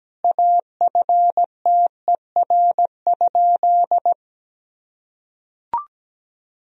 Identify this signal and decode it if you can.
Morse code, heard as a keyed tone, about 17 wpm, 700 Hz. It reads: AFTER?
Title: Sentences from Top 200 Words and Abbreviations 17wpm